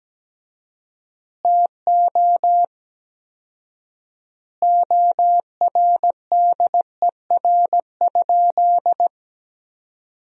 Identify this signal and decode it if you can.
Morse code, heard as a keyed tone, about 17 wpm, 700 Hz. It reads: TO ORDER?
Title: 4x Word-Spacing - Sentences from Top 400 Words - 17wpm